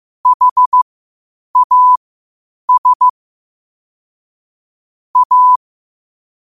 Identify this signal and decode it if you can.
Morse code, heard as a keyed tone, about 15 wpm, spaced as for 8 wpm, 1000 Hz. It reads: HAS A
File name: qotdfiles / 8wpm